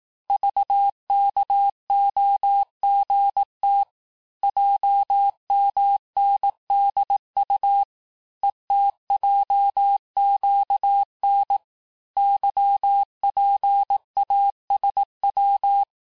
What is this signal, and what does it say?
Morse code, heard as a keyed tone, about 18 wpm, 800 Hz. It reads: VKOGT JMNDU ETJQN YPASW
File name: cours-du-lundi files / cours-1852-texte-chiffre-vitesse-17MPM